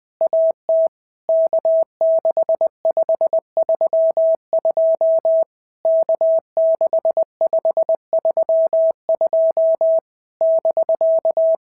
Morse code, heard as a keyed tone, about 20 wpm, 650 Hz. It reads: AT K6532 K6532 <BK>
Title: POTA QSO Practice - 20wpm